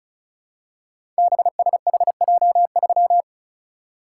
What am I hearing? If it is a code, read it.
Morse code, heard as a keyed tone, about 36 wpm, 700 Hz. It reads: BSHJ3
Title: Code Groups - 5 Letter:Numbers - 35wpm